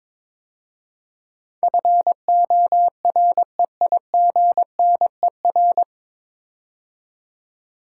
Morse code, heard as a keyed tone, about 22 wpm, 700 Hz. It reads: FOREIGNER
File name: 9 Letter Words - First 3 Letters Given - 22wpm